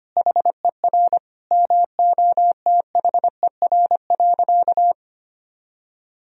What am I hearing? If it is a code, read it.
Morse code, heard as a keyed tone, about 25 wpm, 700 Hz. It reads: HER MOTHER.
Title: Sentences from Top 1000 Words - Review Part 4 - 25wpm